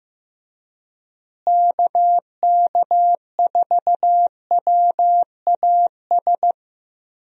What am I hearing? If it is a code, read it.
Morse code, heard as a keyed tone, about 15 wpm, 700 Hz. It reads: KK4WAS
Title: USA Call Signs - 6 Characters - Repeated 3x - 15wpm